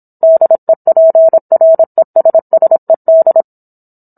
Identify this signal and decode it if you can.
Morse code, heard as a keyed tone, about 27 wpm, 650 Hz. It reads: DEPRESSED